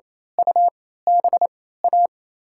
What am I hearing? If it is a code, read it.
Morse code, heard as a keyed tone, about 27 wpm, spaced as for 16 wpm, 700 Hz. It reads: UBA
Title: USA Call Signs - 3x Character Spacing - 28wpm